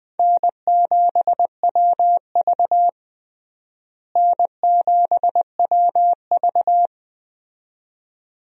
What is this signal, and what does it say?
Morse code, heard as a keyed tone, about 20 wpm, 700 Hz. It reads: N7WV N7WV
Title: USA Call Signs - 4 Characters - Repeated 2x - 20wpm